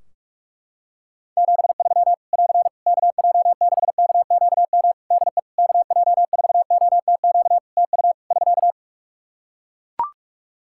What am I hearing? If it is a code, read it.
Morse code, heard as a keyed tone, about 41 wpm, 700 Hz. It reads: 73 <AR> KJ6KQM DE KJ4OTQ TU <SK>
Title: Practice QSO - Exchange 4 - Ending - 45wpm